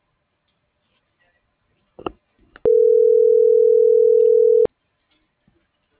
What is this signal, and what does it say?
Ringback tone (North America), 440 + 480 Hz.